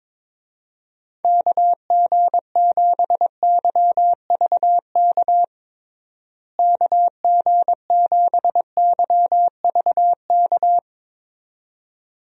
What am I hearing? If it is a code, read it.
Morse code, heard as a keyed tone, about 22 wpm, 700 Hz. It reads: KG7YVK KG7YVK
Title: USA Call Signs - 6 Characters - Repeated 2x - 22wpm